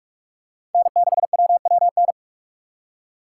Morse code, heard as a keyed tone, about 46 wpm, 700 Hz. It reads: N6WWN
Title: USA Call Signs - 5 Characters - 2x Character Spacing - 45wpm